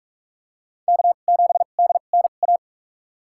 Morse code, heard as a keyed tone, about 45 wpm, spaced as for 32 wpm, 690 Hz. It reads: K7DNA
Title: USA Call Signs - 5 Characters - 2x Character Spacing - 45wpm